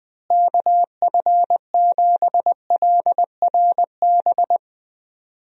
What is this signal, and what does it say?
Morse code, heard as a keyed tone, about 21 wpm, 700 Hz. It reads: KF7LRB
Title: USA Call Signs - 6 Characters - Repeated 2x - 20wpm